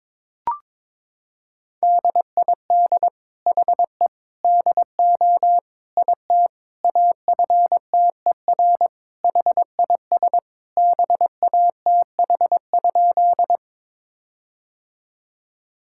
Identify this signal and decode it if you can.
Morse code, heard as a keyed tone, about 22 wpm, 700 Hz. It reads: DID HE DO IT AFTER HIS BATH?
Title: Sentences from Top 1600 Words 22wpm